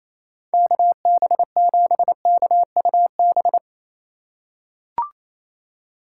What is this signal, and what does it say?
Morse code, heard as a keyed tone, about 28 wpm, 700 Hz. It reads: KB7KUB
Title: USA Call Signs - 6 Characters - 3x Character Spacing - 28wpm